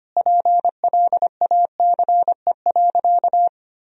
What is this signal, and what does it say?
Morse code, heard as a keyed tone, about 25 wpm, 700 Hz. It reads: PLACE.
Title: Sentences from Top 1000 Words - Review Part 1 - 25wpm